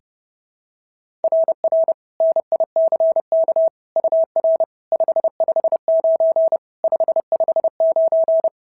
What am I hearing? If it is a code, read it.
Morse code, heard as a keyed tone, about 26 wpm, 650 Hz. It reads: RR NICK UR 559 559
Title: SOTA QSO Practice 30wpm